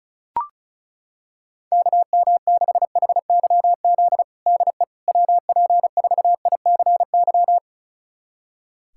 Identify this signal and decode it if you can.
Morse code, heard as a keyed tone, about 36 wpm, 700 Hz. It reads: KM6HYZ DE WP4ICY
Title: USA Callsign DE Callsign - 6 Characters - 35wpm